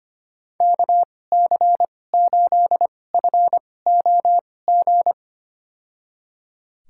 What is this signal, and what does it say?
Morse code, heard as a keyed tone, about 25 wpm, spaced as for 18 wpm, 700 Hz. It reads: KC8FOG
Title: USA Call Signs - 2x Character Spacing - 25wpm